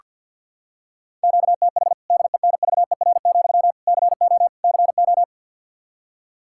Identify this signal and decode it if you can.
Morse code, heard as a keyed tone, about 47 wpm, 700 Hz. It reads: QTH DENVER, CO CO
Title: K1USN SST QSO Practice 50wpm